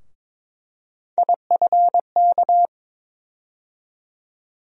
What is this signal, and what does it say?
Morse code, heard as a keyed tone, about 23 wpm, 700 Hz. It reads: IFK
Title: USA Call Sign Postfixes - 3 Letters 22wpm